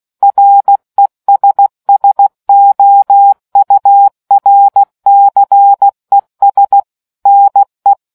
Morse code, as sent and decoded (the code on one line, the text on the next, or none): .-. ... ... --- ..- .-. -.-. ... / -. .
RESSOURCES NE